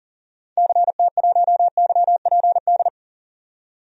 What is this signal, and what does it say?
Morse code, heard as a keyed tone, about 40 wpm, 700 Hz. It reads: CT1YPD
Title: DXCC-100 Call Signs - 6 Characters - 40wpm